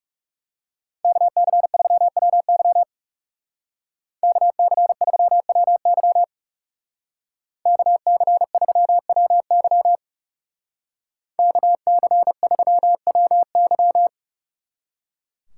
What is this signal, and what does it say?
Morse code, heard as a keyed tone, about 39 wpm, 700 Hz. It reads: KC3WY KC3WY KC3WY KC3WY